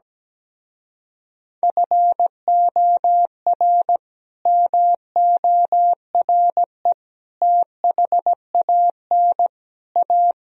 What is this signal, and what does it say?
Morse code, heard as a keyed tone, about 17 wpm, 700 Hz. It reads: FOR MORE THAN A
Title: Speed-Racing - Sets of 4 Words 17wpm